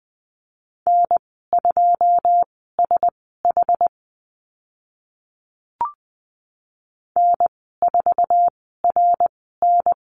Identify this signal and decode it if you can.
Morse code, heard as a keyed tone, about 20 wpm, spaced as for 14 wpm, 700 Hz. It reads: N2SH N4RN
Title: 082 Lesson 14g - Number 2 - Call Signs 20wpm